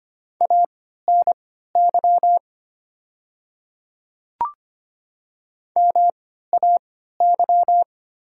Morse code, heard as a keyed tone, about 25 wpm, spaced as for 14 wpm, 700 Hz. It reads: ANY MAY
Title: 3x Character Spacing - Top 500 Words - 25wpm